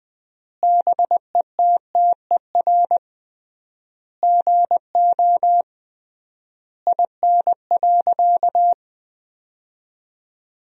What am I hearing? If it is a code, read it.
Morse code, heard as a keyed tone, about 20 wpm, 700 Hz. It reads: BETTER GO IN.
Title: 3x Word-Spacing - Sentences from Top 400 Words - 20wpm